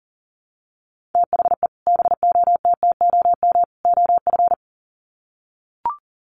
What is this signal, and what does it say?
Morse code, heard as a keyed tone, about 40 wpm, 700 Hz. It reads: THE BOTTOM OF
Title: Sets of 3 Words 40wpm